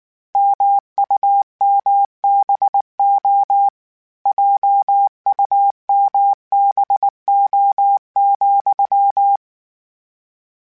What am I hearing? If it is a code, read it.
Morse code, heard as a keyed tone, about 19 wpm, 800 Hz. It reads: MUMBO JUMBO,
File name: Misc / Misc-mp3s / Word Pairs @18